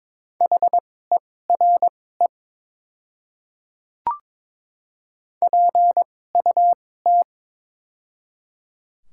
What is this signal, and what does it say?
Morse code, heard as a keyed tone, about 23 wpm, spaced as for 16 wpm, 690 Hz. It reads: HERE PUT
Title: 2x Character Spacing - Top 200 Words - 22wpm